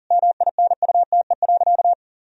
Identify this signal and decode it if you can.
Morse code, heard as a keyed tone, about 40 wpm, 700 Hz. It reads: MINUTE.
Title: Sentences from Top 1000 Words - Review Part 1 - 40wpm